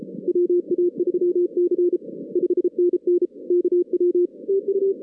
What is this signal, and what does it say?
Morse code, heard as a keyed tone, about 33 wpm, 380 Hz. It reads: WA3C 5NN KW TU
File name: 2016 arrl dx cw / 2016-ARRL-DX-CW-K5TR- - 0-41